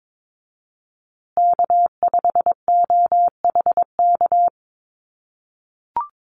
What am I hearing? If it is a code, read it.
Morse code, heard as a keyed tone, about 22 wpm, 700 Hz. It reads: K5OHK
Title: USA Call Signs Repeated 2x 22wpm